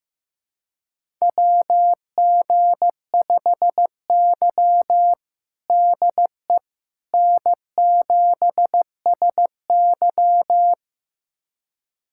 Morse code, heard as a keyed tone, about 15 wpm, 700 Hz. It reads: WG5Y DE N7SY
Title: USA Callsign DE Callsign - 4 Characters - 15wpm